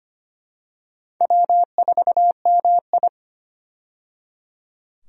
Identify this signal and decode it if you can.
Morse code, heard as a keyed tone, about 25 wpm, 700 Hz. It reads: W4MI
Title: USA Call Signs - 4 Characters - 25wpm